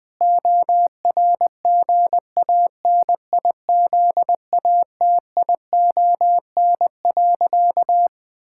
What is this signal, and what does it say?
Morse code, heard as a keyed tone, about 20 wpm, 700 Hz. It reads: ORGANIZATION.
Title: Sentences from Top 1000 Words - Review Part 5 - 20wpm